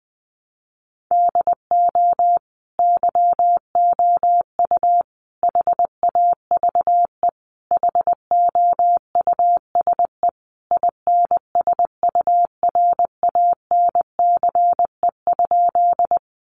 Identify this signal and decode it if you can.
Morse code, heard as a keyed tone, about 20 wpm, 700 Hz. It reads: DO YOU HAVE HOUSE INSURANCE?
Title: Sentences from Top 1100 Words 20wpm